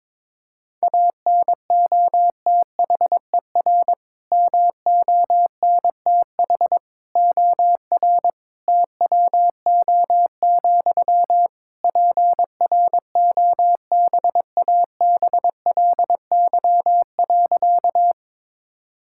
Morse code, heard as a keyed tone, about 22 wpm, 700 Hz. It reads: ANOTHER MONTH OR TWO, PROBABLY.
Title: Sentences from Top 1000 Words - Review Part 5 - 22wpm